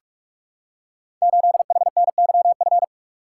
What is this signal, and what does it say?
Morse code, heard as a keyed tone, about 44 wpm, 700 Hz. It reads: 8HNYF
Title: Code Groups - 5 Letter:Numbers - 45wpm